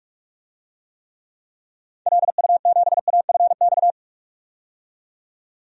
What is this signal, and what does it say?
Morse code, heard as a keyed tone, about 46 wpm, 700 Hz. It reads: LU7AFX
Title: DXCC-100 Call Signs - 6 Characters - 45wpm